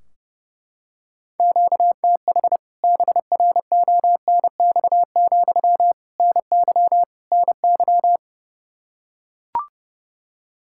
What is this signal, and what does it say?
Morse code, heard as a keyed tone, about 32 wpm, 700 Hz. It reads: QTH BRONX, NY NY
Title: K1USN SST QSO Practice 30wpm